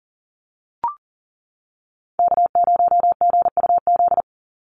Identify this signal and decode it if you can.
Morse code, heard as a keyed tone, about 41 wpm, 700 Hz. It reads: K0GUZ
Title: Call Signs - 40wpm